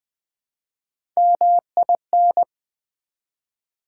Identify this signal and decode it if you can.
Morse code, heard as a keyed tone, about 20 wpm, 700 Hz. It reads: MIN
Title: Build A Word - 1 Letter At A Time - 3 to 5-Letter Words - 20wpm